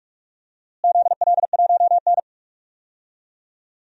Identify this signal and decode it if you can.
Morse code, heard as a keyed tone, about 44 wpm, 700 Hz. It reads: ZL1N
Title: DXCC-100 Call Signs - 4 Characters - Repeated 2x - 45wpm